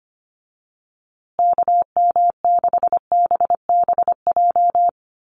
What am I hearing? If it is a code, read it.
Morse code, heard as a keyed tone, about 25 wpm, 700 Hz. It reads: KM6BBJ
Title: USA Call Signs Repeated 3x 25wpm